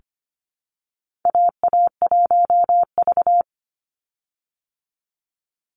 Morse code, heard as a keyed tone, about 25 wpm, 700 Hz. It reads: AA1V
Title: Call Signs Repeated 2x - 25wpm